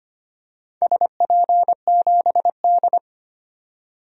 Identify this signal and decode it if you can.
Morse code, heard as a keyed tone, about 25 wpm, 700 Hz. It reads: SP7D
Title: DXCC-100 Call Signs - 4 Characters - 25wpm